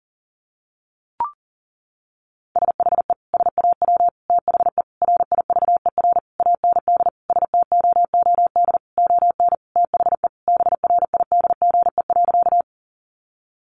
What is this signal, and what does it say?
Morse code, heard as a keyed tone, about 37 wpm, 700 Hz. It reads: SHE SAW THE RIVER AND STOOD ON THE BRIDGE.
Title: Sentences from Top 1200 Words 40wpm